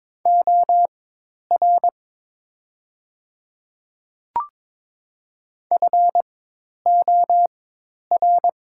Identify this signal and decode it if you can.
Morse code, heard as a keyed tone, about 22 wpm, spaced as for 10 wpm, 700 Hz. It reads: OR FOR
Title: 4x Character Spacing - Top 500 Words - 22wpm